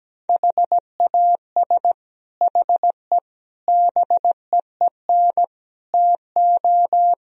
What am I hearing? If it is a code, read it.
Morse code, heard as a keyed tone, about 17 wpm, 700 Hz. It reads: HAS HE BEEN TO
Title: Sentences from Top 1000 Words - Review Part 1 - 17wpm